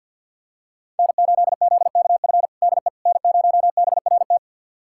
Morse code, heard as a keyed tone, about 48 wpm, 690 Hz. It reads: N8ZKF DE N0BRT
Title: USA Callsign DE Callsign - 5 Characters - 50wpm